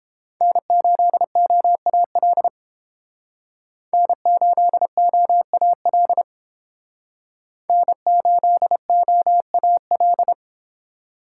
Morse code, heard as a keyed tone, about 29 wpm, 700 Hz. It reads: N8OAL N8OAL N8OAL